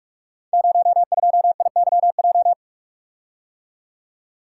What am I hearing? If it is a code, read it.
Morse code, heard as a keyed tone, about 44 wpm, 700 Hz. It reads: 02IYJ
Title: Code Groups - 5 Letter:Numbers - 45wpm